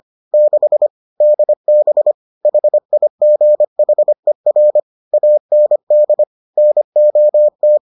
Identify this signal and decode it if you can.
Morse code, heard as a keyed tone, about 25 wpm, 600 Hz. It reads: 6 DB HIGHER AND NOT